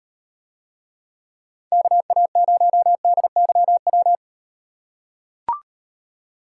Morse code, heard as a keyed tone, about 36 wpm, 700 Hz. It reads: KA0DYW